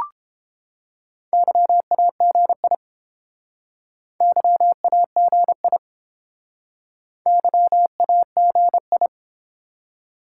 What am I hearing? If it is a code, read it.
Morse code, heard as a keyed tone, about 30 wpm, 700 Hz. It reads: YAGI YAGI YAGI